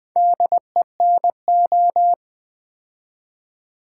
Morse code, heard as a keyed tone, about 20 wpm, 700 Hz. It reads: DENO
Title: Build A Word - 2 Letters At A Time - 8-Letter Words - 20wpm